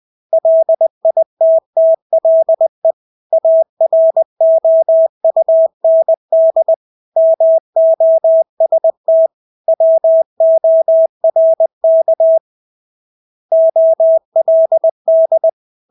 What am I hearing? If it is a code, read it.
Morse code, heard as a keyed tone, about 20 wpm, 650 Hz. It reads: LITTLE AROUND MOST WORK OLD
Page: Engelske ord 20 wpm